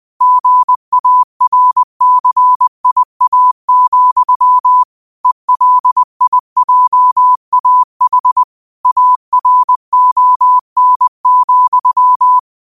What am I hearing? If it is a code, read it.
Morse code, heard as a keyed tone, about 20 wpm, 1000 Hz. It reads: GARCIA, ELIJAH ARON,
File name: qotdfiles / 20wpm